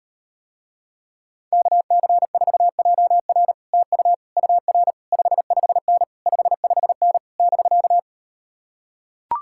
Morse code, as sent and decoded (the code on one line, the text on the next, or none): -.- -.-. ....- .--- .-. / - ..- / ..- .-. / ..... ..... -. / ..... ..... -. / -...-.-
KC4JR TU UR 55N 55N <BK>